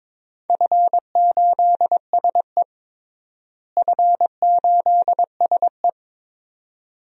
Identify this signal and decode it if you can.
Morse code, heard as a keyed tone, about 22 wpm, 700 Hz. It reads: F8SE F8SE
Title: DXCC-100 Call Signs - 4 Characters - Repeated 2x - 22wpm